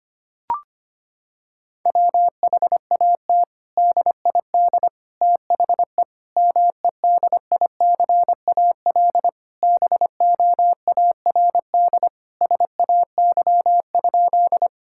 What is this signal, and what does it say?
Morse code, heard as a keyed tone, about 25 wpm, 700 Hz. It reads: WHAT DID THE MEDICAL BOARD SAY?
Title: Sentences from Top 1000 Words - Review Part 5 - 25wpm